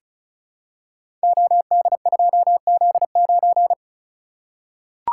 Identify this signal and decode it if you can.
Morse code, heard as a keyed tone, about 36 wpm, 700 Hz. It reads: OD2Z9
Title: Code Groups - 5 Letter:Numbers - 35wpm